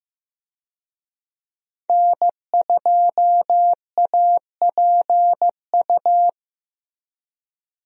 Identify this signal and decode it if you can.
Morse code, heard as a keyed tone, about 15 wpm, 700 Hz. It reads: N2APU
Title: USA Call Signs - 5 Characters - Repeated 2x - 15wpm